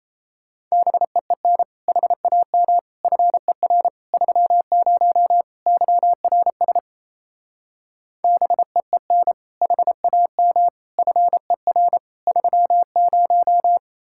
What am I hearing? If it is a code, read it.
Morse code, heard as a keyed tone, about 36 wpm, 700 Hz. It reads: BEEN HAM FER 30 YRS BEEN HAM FER 30